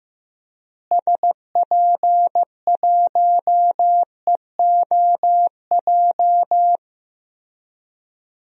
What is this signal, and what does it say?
Morse code, heard as a keyed tone, about 15 wpm, 700 Hz. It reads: SP1EOJ
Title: DXCC-100 Call Signs - 6 Characters - Repeated 3x - 15wpm